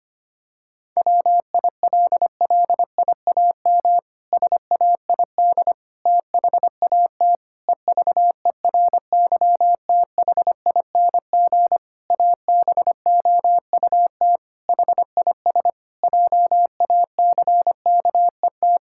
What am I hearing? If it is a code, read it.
Morse code, heard as a keyed tone, about 25 wpm, 700 Hz. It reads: WILLIAM SAID THAT EVERYTHING ABOUT HIS JACKET
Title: Pangram Challenge v2 - 25wpm